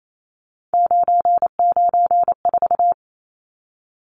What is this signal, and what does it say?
Morse code, heard as a keyed tone, about 28 wpm, 700 Hz. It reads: 994
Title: Numbers - 3-Digit 28wpm